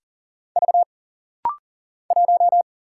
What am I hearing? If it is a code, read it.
Morse code, heard as a keyed tone, about 40 wpm, 700 Hz. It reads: V 1
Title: Single Letter-Number - Rapid-Fire 40wpm